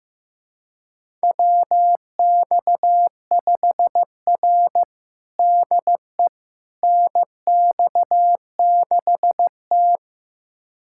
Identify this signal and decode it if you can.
Morse code, heard as a keyed tone, about 15 wpm, 700 Hz. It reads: WX5R DE NX6T
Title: USA Callsign DE Callsign - 4 Characters - 15wpm